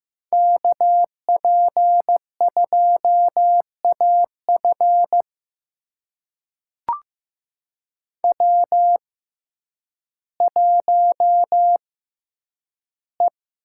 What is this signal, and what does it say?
Morse code, heard as a keyed tone, about 15 wpm, spaced as for 9 wpm, 700 Hz. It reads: KP2AF W1E
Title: USA Call Signs - 5 Characters - 6x Character Spacing - 15wpm